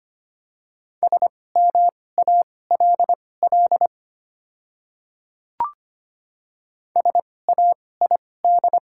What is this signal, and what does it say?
Morse code, heard as a keyed tone, about 25 wpm, spaced as for 18 wpm, 690 Hz. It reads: SMALL SAID
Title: 2x Character Spacing - Top 200 Words - 25wpm